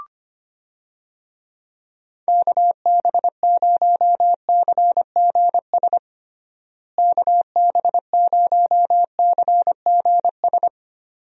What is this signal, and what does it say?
Morse code, heard as a keyed tone, about 25 wpm, 700 Hz. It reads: KB0CGS KB0CGS
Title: USA Call Signs - 6 Characters - Repeated 2x - 25wpm